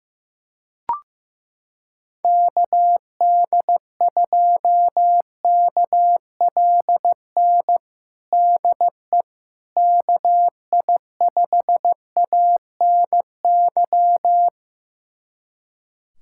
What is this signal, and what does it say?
Morse code, heard as a keyed tone, about 15 wpm, 700 Hz. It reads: KD2KLN DE KI5ANY
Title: USA Callsign DE Callsign - 6 Characters - 15wpm